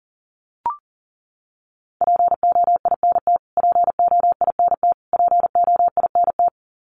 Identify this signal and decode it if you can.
Morse code, heard as a keyed tone, about 40 wpm, 700 Hz. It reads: POINT POINT POINT
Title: Top 200 Words Repeated 40wpm